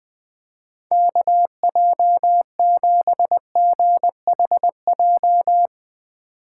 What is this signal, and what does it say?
Morse code, heard as a keyed tone, about 20 wpm, 700 Hz. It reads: KJ7GHJ
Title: Speed Racing - USA 2x3 Call Signs 20wpm